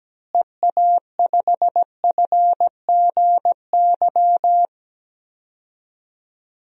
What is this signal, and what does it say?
Morse code, heard as a keyed tone, about 17 wpm, 700 Hz. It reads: EA5FGY
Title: DXCC-100 Call Signs - 6 Characters - Repeated 2x - 17wpm